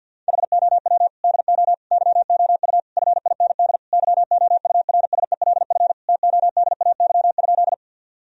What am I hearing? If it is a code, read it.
Morse code, heard as a keyed tone, about 50 wpm, 700 Hz. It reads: HOW DO YOU FIND YOURSELF TODAY?
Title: Sentences from Top 1000 Words - Review Part 4 - 50wpm